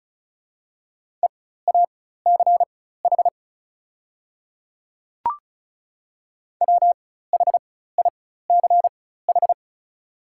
Morse code, heard as a keyed tone, about 36 wpm, spaced as for 16 wpm, 700 Hz. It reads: EACH WHICH
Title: 4x Character Spacing - Top 100 Words - 35wpm